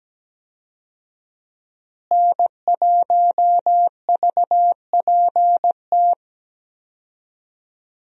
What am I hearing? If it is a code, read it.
Morse code, heard as a keyed tone, about 17 wpm, 700 Hz. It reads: N1VPT
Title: USA Call Signs - 5 Characters - Repeated 3x - 17wpm